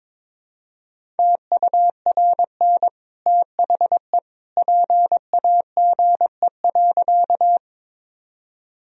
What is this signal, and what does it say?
Morse code, heard as a keyed tone, about 22 wpm, 700 Hz. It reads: TURN THE PAGE.
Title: Sentences from Top 1000 Words - Review Part 1 - 22wpm